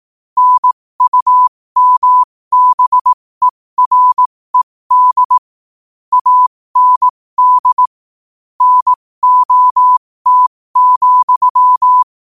Morse code, heard as a keyed tone, about 18 wpm, spaced as for 15 wpm, 1000 Hz. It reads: NUMBERED AND NOT,
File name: qotdfiles / 15wpm